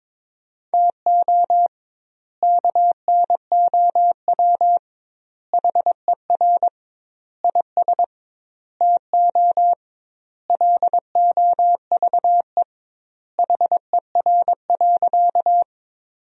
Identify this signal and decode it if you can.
Morse code, heard as a keyed tone, about 22 wpm, 700 Hz. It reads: TO KNOW HER IS TO LOVE HER.
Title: 2x Word-Spacing - Sentences from Top 400 Words - 22wpm